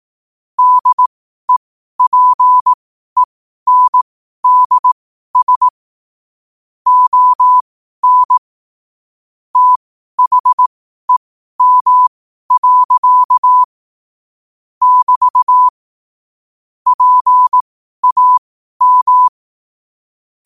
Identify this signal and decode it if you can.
Morse code, heard as a keyed tone, about 18 wpm, spaced as for 12 wpm, 1000 Hz. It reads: DEPENDS ON THEM. <BT> PAM
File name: qotdfiles / 12wpm